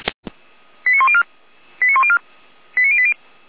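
Five-tone selective call (ZVEI), heard as quick runs of five tones, tones of about 70 ms. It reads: 89104 89104 89080
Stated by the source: ZVEI five-tone selective call 89020 received by SDR.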